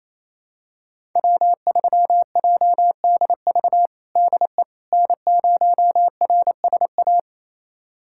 Morse code, heard as a keyed tone, about 28 wpm, 700 Hz. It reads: W3JDV DE N0RSA
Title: USA Callsign DE Callsign - 5 Characters - 28wpm